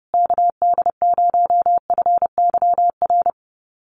Morse code, heard as a keyed tone, about 30 wpm, 700 Hz. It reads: KD0FYR
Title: USA Call Signs Repeated 2x 30wpm